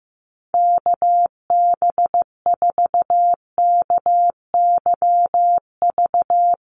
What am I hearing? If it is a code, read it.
Morse code, heard as a keyed tone, about 15 wpm, 700 Hz. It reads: KB4KYV
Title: USA Call Signs 15wpm